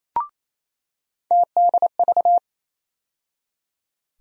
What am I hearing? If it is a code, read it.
Morse code, heard as a keyed tone, about 28 wpm, 700 Hz. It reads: TDV
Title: USA Call Sign Postfixes - 3 Letters 28wpm